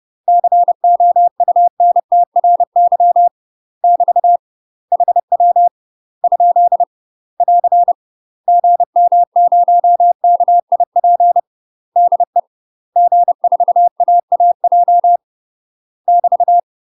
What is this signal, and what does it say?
Morse code, heard as a keyed tone, about 30 wpm, 700 Hz. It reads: COUNTRY <BT> HW ? <AR> GM0KIP DE G4AAJ <BT>
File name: UK MASTER QSOs TRAINING / 30WPM / UK MASTER QSOs 0035